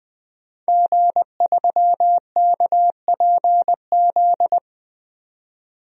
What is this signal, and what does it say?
Morse code, heard as a keyed tone, about 20 wpm, 700 Hz. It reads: G3KPZ
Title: DXCC-100 Call Signs - 5 Characters - 20wpm